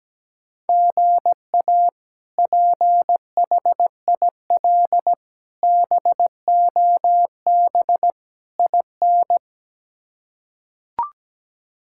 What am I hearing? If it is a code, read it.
Morse code, heard as a keyed tone, about 17 wpm, 700 Hz. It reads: GA PHIL BOB IN
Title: K1USN SST QSO Practice - 17wpm